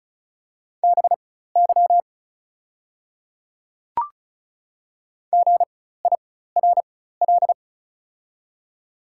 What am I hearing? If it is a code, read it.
Morse code, heard as a keyed tone, about 36 wpm, spaced as for 16 wpm, 700 Hz. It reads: BY GIRL
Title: 4x Character Spacing - Top 500 Words - 35wpm